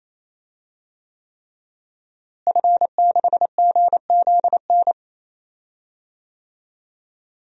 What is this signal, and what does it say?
Morse code, heard as a keyed tone, about 28 wpm, 700 Hz. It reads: F6GZN
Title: DXCC-100 Call Signs - 5 Characters - Repeated 3x - 28wpm